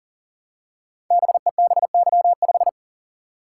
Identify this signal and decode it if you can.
Morse code, heard as a keyed tone, about 38 wpm, 700 Hz. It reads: BEBY5
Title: Code Groups - 5 Letter:Numbers - 40wpm